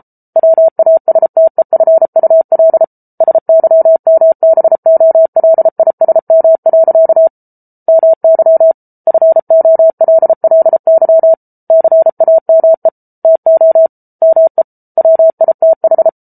Morse code, heard as a keyed tone, about 34 wpm, 650 Hz. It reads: WASTEFUL SYMBOLISM. MY FOLLY CAME TO ME WITH